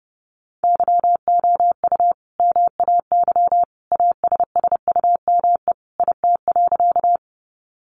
Morse code, heard as a keyed tone, about 28 wpm, 700 Hz. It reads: YOU MAY ASSUME IT.
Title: Sentences from Top 1500 Words 30wpm